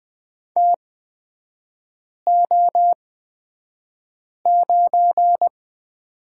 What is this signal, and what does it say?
Morse code, heard as a keyed tone, about 20 wpm, spaced as for 5 wpm, 700 Hz. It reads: TO9
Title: 224 Lesson 36e - BT - Three Characters 20wpm